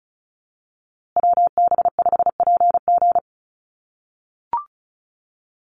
Morse code, heard as a keyed tone, about 34 wpm, 700 Hz. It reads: WB5PG
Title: USA Call Signs Repeated 2x 35wpm